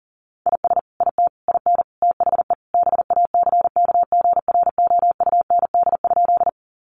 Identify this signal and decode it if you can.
Morse code, heard as a keyed tone, about 40 wpm, 700 Hz. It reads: IS IT IN THE BACKGROUND?
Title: Sentences from Top 1300 Words 40wpm